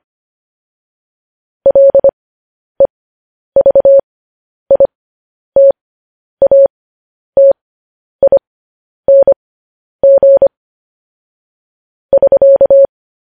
Morse code, heard as a keyed tone, about 25 wpm, spaced as for 10 wpm, 550 Hz. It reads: LEVITATING <SK>
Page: news-headlines-in-morse-code-at-10-wpm